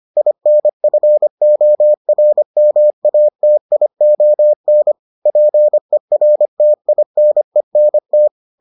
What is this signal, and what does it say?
Morse code, heard as a keyed tone, about 25 wpm, 600 Hz. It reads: INFORMATION PERTINENT